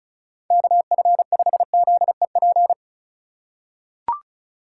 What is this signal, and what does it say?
Morse code, heard as a keyed tone, about 36 wpm, 700 Hz. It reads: KF5ZEP
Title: USA Call Signs - 6 Characters - 3x Character Spacing - 35wpm